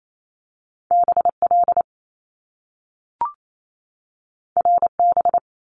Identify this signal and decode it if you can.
Morse code, heard as a keyed tone, about 28 wpm, 700 Hz. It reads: BL RB
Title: Top Two Letter Combinations in Words 28wpm